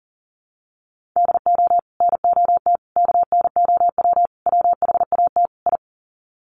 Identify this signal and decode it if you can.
Morse code, heard as a keyed tone, about 40 wpm, 700 Hz. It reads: DO NOT KNOW WHAT I
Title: Sets of 5 Words 40wpm